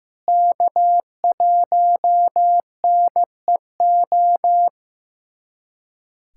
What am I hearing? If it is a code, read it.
Morse code, heard as a keyed tone, about 15 wpm, 700 Hz. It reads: K1NEO
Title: USA Callsign DE Callsign - 5 Characters - 15wpm